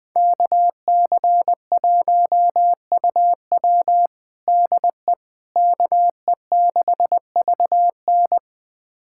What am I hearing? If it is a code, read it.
Morse code, heard as a keyed tone, about 20 wpm, 700 Hz. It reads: KC1UW DE KE6VN
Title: USA Callsign DE Callsign - 5 Characters - 20wpm